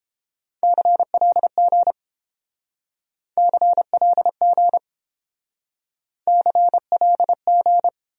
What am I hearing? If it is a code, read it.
Morse code, heard as a keyed tone, about 30 wpm, 700 Hz. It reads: CLG CLG CLG